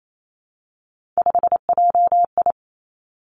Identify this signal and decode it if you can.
Morse code, heard as a keyed tone, about 28 wpm, 700 Hz. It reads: 5JI
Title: Three Challenging Letter-Number Combinations 28wpm